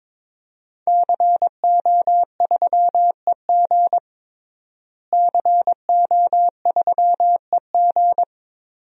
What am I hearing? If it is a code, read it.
Morse code, heard as a keyed tone, about 22 wpm, 700 Hz. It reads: CO3EG CO3EG
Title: DXCC-100 Call Signs - 5 Characters - Repeated 2x - 22wpm